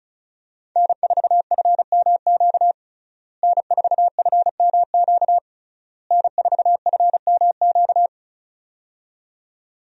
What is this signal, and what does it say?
Morse code, heard as a keyed tone, about 36 wpm, 700 Hz. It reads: N4FMQ N4FMQ N4FMQ
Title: USA Call Signs - 5 Characters - Repeated 3x - 35wpm